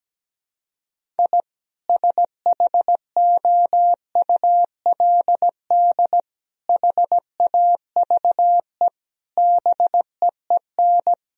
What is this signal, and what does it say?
Morse code, heard as a keyed tone, about 17 wpm, 700 Hz. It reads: I SHOULD HAVE BEEN
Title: Sentences from Top 1000 Words - Review Part 1 - 17wpm